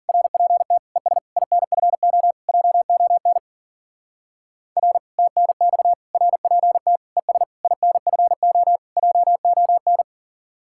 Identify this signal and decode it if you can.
Morse code, heard as a keyed tone, about 47 wpm, 700 Hz. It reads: RPT ES INFO JON R TNX RPT ES INFO JON
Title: Speed-Racing Common QSO Phrases Encore - 40wpm